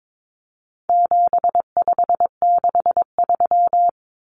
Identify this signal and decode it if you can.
Morse code, heard as a keyed tone, about 22 wpm, 700 Hz. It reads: 7563
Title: Numbers - 4-Digit 22wpm